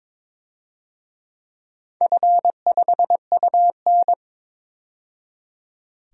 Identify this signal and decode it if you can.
Morse code, heard as a keyed tone, about 22 wpm, 700 Hz. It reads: F5UN
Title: DXCC-100 Call Signs - 4 Characters - 22wpm